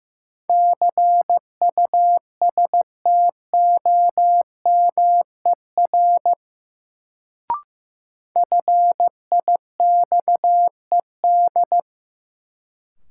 Morse code, heard as a keyed tone, about 15 wpm, 700 Hz. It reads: CUSTOMER FIXED
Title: Sentences from Top 1100 Words - New Word Drill - 15wpm